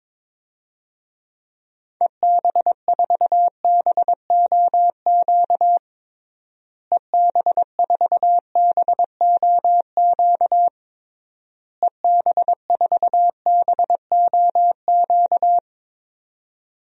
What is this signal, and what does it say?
Morse code, heard as a keyed tone, about 22 wpm, 700 Hz. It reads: EB4BOQ EB4BOQ EB4BOQ
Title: DXCC-100 Call Signs - 6 Characters - Repeated 3x - 22wpm